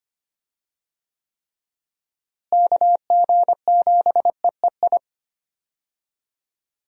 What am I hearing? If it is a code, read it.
Morse code, heard as a keyed tone, about 25 wpm, 700 Hz. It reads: KG7EEI
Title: USA Call Signs - 6 Characters - Repeated 2x - 25wpm